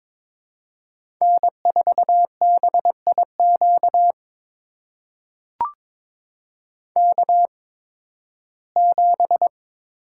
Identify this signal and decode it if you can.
Morse code, heard as a keyed tone, about 22 wpm, spaced as for 14 wpm, 700 Hz. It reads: N4BIQ K7
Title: USA Call Signs - 5 Characters - 8x Character Spacing - 22wpm